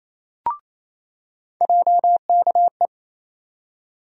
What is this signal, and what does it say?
Morse code, heard as a keyed tone, about 28 wpm, 700 Hz. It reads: JKE
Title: USA Call Sign Postfixes - 3 Letters 28wpm